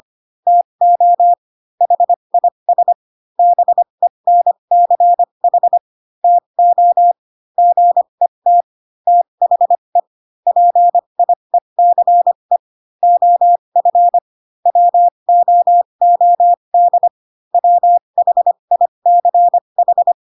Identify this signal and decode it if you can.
Morse code, heard as a keyed tone, about 25 wpm, spaced as for 22 wpm, 700 Hz. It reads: TO HIS BENCH TO GET THE PIECE OF WOOD WHICH
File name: Pinocchio / Pinocchio ch2 18 25 ik1hge